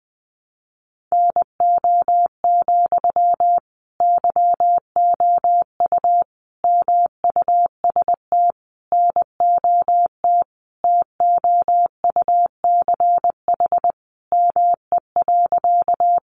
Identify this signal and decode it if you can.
Morse code, heard as a keyed tone, about 20 wpm, 700 Hz. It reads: NO, YOU MUST NOT TOUCH ME.
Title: Sentences from Top 1200 Words 20wpm